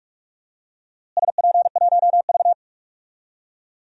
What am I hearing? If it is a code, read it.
Morse code, heard as a keyed tone, about 46 wpm, 700 Hz. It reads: SP1V
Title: DXCC-100 Call Signs - 4 Characters - Repeated 3x - 45wpm